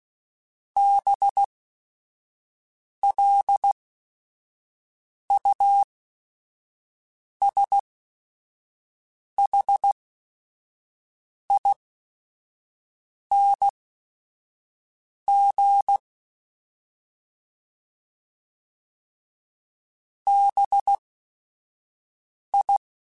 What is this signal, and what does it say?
Morse code, heard as a keyed tone, about 16 wpm, spaced as for 5 wpm, 800 Hz. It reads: BLUSHING BI